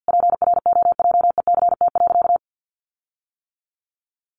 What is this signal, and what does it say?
Morse code, heard as a keyed tone, about 50 wpm, 700 Hz. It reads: PROJECT.
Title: Sentences from Top 1400 Words 50wpm